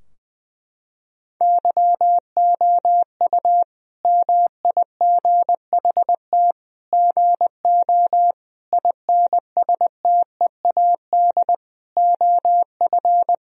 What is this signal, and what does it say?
Morse code, heard as a keyed tone, about 21 wpm, 700 Hz. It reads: YOU MIGHT GO INSTEAD OF
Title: Sentences from Top 1000 Words - Review Part 4 - 20wpm